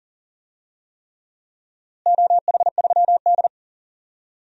Morse code, heard as a keyed tone, about 38 wpm, 700 Hz. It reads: OH3D
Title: DXCC-100 Call Signs - 4 Characters - Repeated 3x - 40wpm